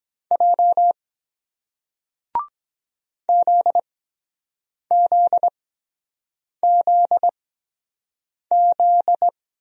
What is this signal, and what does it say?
Morse code, heard as a keyed tone, about 23 wpm, spaced as for 7 wpm, 700 Hz. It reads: J ZZZZ